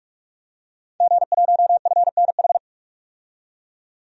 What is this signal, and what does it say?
Morse code, heard as a keyed tone, about 47 wpm, 700 Hz. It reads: G1FNH
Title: DXCC-100 Call Signs - 5 Characters - 45wpm